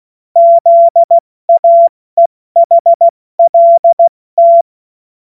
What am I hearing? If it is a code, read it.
Morse code, heard as a keyed tone, about 16 wpm, 680 Hz. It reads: ZAEHLT
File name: Morsekurs / Witze / Witz 11. Mai 2021, 70 F80